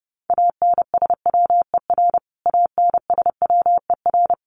Morse code, heard as a keyed tone, about 30 wpm, 700 Hz. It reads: ANSWER ANSWER
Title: Top 500 Words Repeated 30wpm